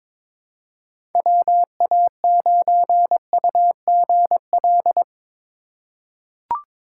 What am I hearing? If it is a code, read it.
Morse code, heard as a keyed tone, about 22 wpm, 700 Hz. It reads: WA9UGL
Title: USA Call Signs - 4x Character Spacing - 22wpm